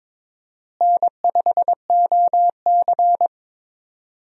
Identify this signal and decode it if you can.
Morse code, heard as a keyed tone, about 22 wpm, 700 Hz. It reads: N5OC
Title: USA Call Signs - 4 Characters - Repeated 2x - 22wpm